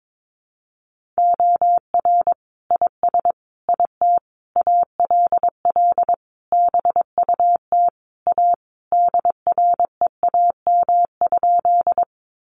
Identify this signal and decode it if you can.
Morse code, heard as a keyed tone, about 22 wpm, 700 Hz. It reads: OR IS IT ALL BUT A DREAM?
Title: Sentences from Top 1500 Words 22wpm